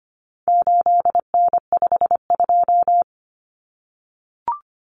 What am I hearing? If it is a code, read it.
Morse code, heard as a keyed tone, about 25 wpm, 700 Hz. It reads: 8N52
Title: Common Cut Numbers - 4-Digit 25wpm